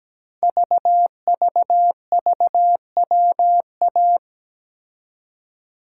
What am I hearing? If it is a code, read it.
Morse code, heard as a keyed tone, about 17 wpm, 700 Hz. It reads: VVVWA
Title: Code Groups - 5 Letters - 17wpm